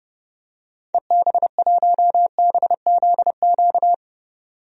Morse code, heard as a keyed tone, about 32 wpm, 700 Hz. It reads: EB1BZQ
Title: DXCC-100 Call Signs - 6 Characters - 30wpm